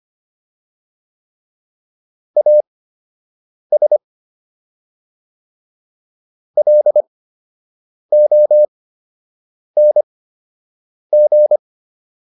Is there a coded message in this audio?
Yes: Morse code from 2.4 to 11.6 s, heard as a keyed tone, about 25 wpm, spaced as for 7 wpm, 600 Hz. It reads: AS LONG